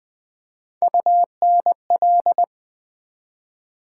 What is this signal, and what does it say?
Morse code, heard as a keyed tone, about 20 wpm, 700 Hz. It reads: UNL
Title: Build A Word - 3 Letters At A Time - 8-Letter Words - 20wpm